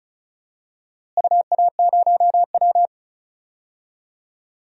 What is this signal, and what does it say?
Morse code, heard as a keyed tone, about 34 wpm, 700 Hz. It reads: UA0W
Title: DXCC-100 Call Signs - 4 Characters - Repeated 2x - 35wpm